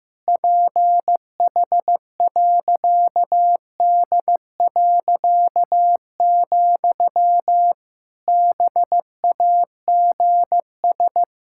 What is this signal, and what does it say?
Morse code, heard as a keyed tone, about 16 wpm, 700 Hz. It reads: PH.D., BAGS
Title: Pangram Challenge v2 - 15wpm